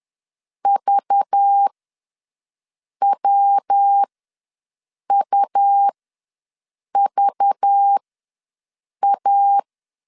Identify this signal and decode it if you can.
Morse code, heard as a keyed tone, about 10 wpm, spaced as for 6 wpm, 800 Hz. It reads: VWUVA
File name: LEZIONI / 5 LBVC / 5 LBV ws 10wpm 3645